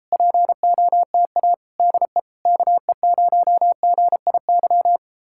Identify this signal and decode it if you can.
Morse code, heard as a keyed tone, about 38 wpm, 700 Hz. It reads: POTA DE KE0GIY